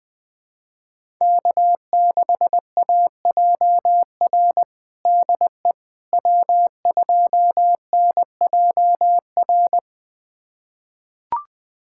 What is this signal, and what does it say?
Morse code, heard as a keyed tone, about 21 wpm, 700 Hz. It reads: K6AJR DE W2NJR
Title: USA Callsign DE Callsign - 5 Characters - 20wpm